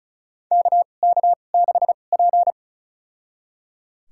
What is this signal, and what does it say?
Morse code, heard as a keyed tone, about 34 wpm, spaced as for 25 wpm, 700 Hz. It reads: KK6P
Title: USA Call Signs - 4 Characters - 2x Character Spacing - 35wpm